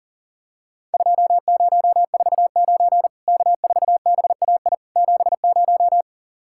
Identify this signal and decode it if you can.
Morse code, heard as a keyed tone, about 40 wpm, 700 Hz. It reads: 2049 K4BAI 70
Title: Worked All Europe DX Contest - QTC Exchange - 40wpm